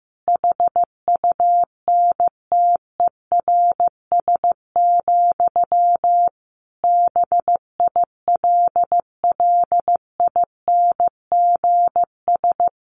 Morse code, heard as a keyed tone, about 15 wpm, 700 Hz. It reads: HUNTERS, BILLINGS
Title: A Sound Of Thunder 15wpm